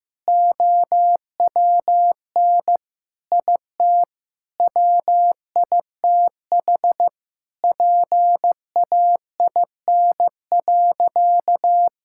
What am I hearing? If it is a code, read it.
Morse code, heard as a keyed tone, about 16 wpm, 700 Hz. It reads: OWN IT WITH PAIN.
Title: Sentences from Top 1600 Words 15wpm